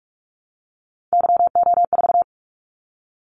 Morse code, heard as a keyed tone, about 46 wpm, 700 Hz. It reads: YO4
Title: Callsign Prefixes for Top 25 DXCC - 45wpm